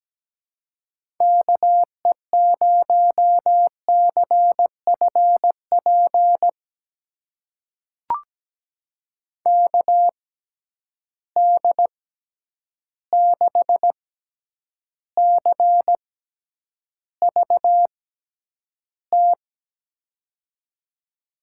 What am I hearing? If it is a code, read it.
Morse code, heard as a keyed tone, about 17 wpm, spaced as for 9 wpm, 700 Hz. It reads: KE0CFP KD6CVT
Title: USA Call Signs - 6 Characters - 6x Character Spacing - 17wpm